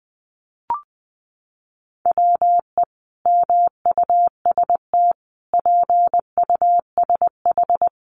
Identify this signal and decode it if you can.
Morse code, heard as a keyed tone, about 20 wpm, 700 Hz. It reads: WE MUST PUSH
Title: Sentences from Top 1300 Words 20wpm